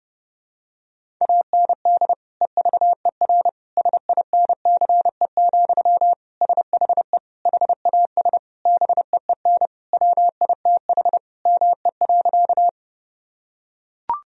Morse code, heard as a keyed tone, about 32 wpm, 700 Hz. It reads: AND EVER SINCE, SHE HAS BEEN WITH ME.
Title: Sentences from Top 1000 Words - Review Part 1 - 30wpm